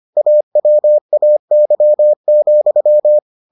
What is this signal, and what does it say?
Morse code, heard as a keyed tone, about 25 wpm, 600 Hz. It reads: AWAY,